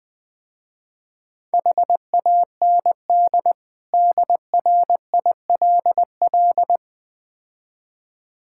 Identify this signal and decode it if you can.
Morse code, heard as a keyed tone, about 20 wpm, 700 Hz. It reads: HAND DRILL
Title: Easy Words - Speed-Racing with Hints - 20wpm